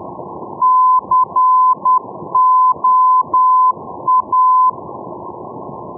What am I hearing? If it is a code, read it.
Morse code, heard as a keyed tone, about 10 wpm, 1020 Hz. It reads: COA
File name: europe / e-601 ndb la coruna coa-401 np lacoruna 2016-05